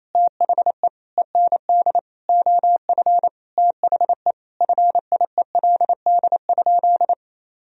Morse code, heard as a keyed tone, about 28 wpm, 700 Hz. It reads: THE END OF THE FIELD?
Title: Sentences from Top 1000 Words - Review Part 2 - 28wpm